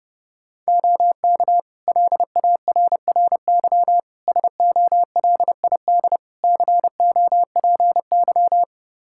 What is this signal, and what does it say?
Morse code, heard as a keyed tone, about 30 wpm, 700 Hz. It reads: OK LARRY SOLID COPY
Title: Practice QSO - Exchange 3 - More Info - 30wpm